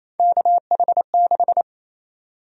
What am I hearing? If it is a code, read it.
Morse code, heard as a keyed tone, about 28 wpm, 700 Hz. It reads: KH6
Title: USA Call Sign Prefixes - 28wpm